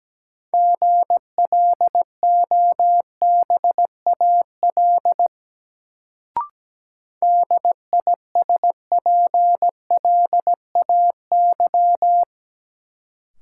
Morse code, heard as a keyed tone, about 17 wpm, 700 Hz. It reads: GLOBAL DISPLAY
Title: Sentences from Top 1200 Words - New Word Drill Alt - 17wpm